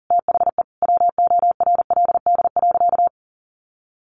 Morse code, heard as a keyed tone, about 40 wpm, 700 Hz. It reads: THE WORLD.
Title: Sentences from Top 1300 Words 40wpm